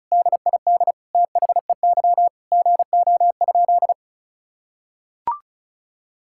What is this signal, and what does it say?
Morse code, heard as a keyed tone, about 36 wpm, 700 Hz. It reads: DID THEY GO?
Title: Sentences from Top 1600 Words 35wpm